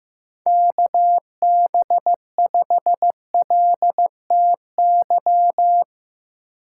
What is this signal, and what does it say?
Morse code, heard as a keyed tone, about 15 wpm, 700 Hz. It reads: KB5LTY
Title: USA Call Signs - 6 Characters - 4x Character Spacing - 15wpm